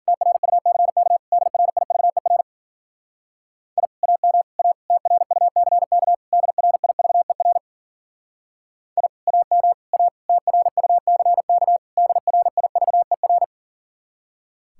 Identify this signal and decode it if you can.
Morse code, heard as a keyed tone, about 51 wpm, 700 Hz. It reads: TRUCK DRIVER I AM A TRUCK DRIVER I AM A TRUCK DRIVER